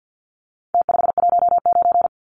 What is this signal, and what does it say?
Morse code, heard as a keyed tone, about 48 wpm, 700 Hz. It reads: T519
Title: Numbers - Contest Style - 4-Digit 50wpm